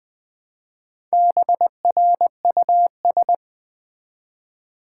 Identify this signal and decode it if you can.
Morse code, heard as a keyed tone, about 20 wpm, 700 Hz. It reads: BRUS
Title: Build A Word - 1 Letter At A Time - 3 to 5-Letter Words - 20wpm